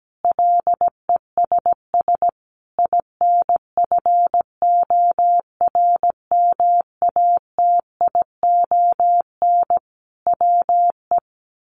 Morse code, heard as a keyed tone, about 17 wpm, 700 Hz. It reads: LESS INFORMATION WE
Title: Sentences from Top 1300 Words 17wpm